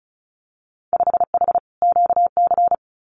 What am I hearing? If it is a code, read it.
Morse code, heard as a keyed tone, about 36 wpm, 700 Hz. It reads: 5H QC
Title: Winter Field Day Exchanges Encore 35wpm